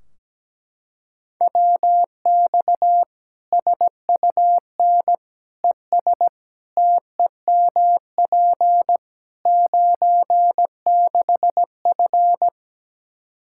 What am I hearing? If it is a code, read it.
Morse code, heard as a keyed tone, about 17 wpm, 700 Hz. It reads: WX SUN ES TEMP 96F
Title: Practice QSO - Exchange 2 - Info - 17wpm